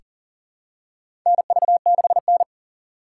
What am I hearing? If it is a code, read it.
Morse code, heard as a keyed tone, about 40 wpm, 700 Hz. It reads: NV6N
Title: USA Call Signs - 4 Characters - 4x Character Spacing - 40wpm